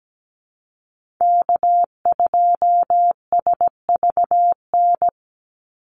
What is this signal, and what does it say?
Morse code, heard as a keyed tone, about 17 wpm, 700 Hz. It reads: K2SVN
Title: USA Call Signs 17wpm